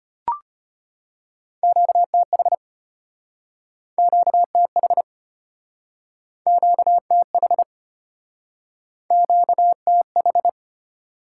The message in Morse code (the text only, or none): QTH QTH QTH QTH